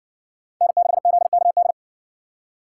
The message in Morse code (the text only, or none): N6ZKD